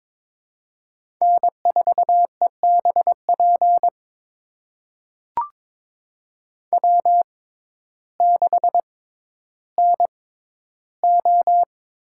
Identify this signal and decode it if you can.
Morse code, heard as a keyed tone, about 22 wpm, spaced as for 12 wpm, 700 Hz. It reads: N4EBP W6NO
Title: USA Call Signs - 6x Character Spacing - 22wpm